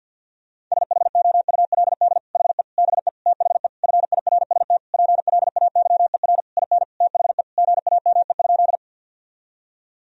Morse code, heard as a keyed tone, about 52 wpm, 700 Hz. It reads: SHOULD HE BE THE FIRST PLAYER IN THE GAME?
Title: Sentences from Top 1000 Words - Review Part 4 - 50wpm